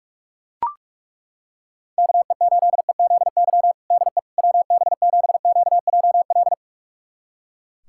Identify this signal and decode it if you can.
Morse code, heard as a keyed tone, about 41 wpm, 700 Hz. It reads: KE8EZY DE WB7QJL
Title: USA Callsign DE Callsign - 6 Characters - 45wpm